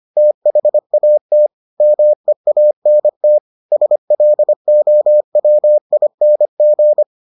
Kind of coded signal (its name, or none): Morse code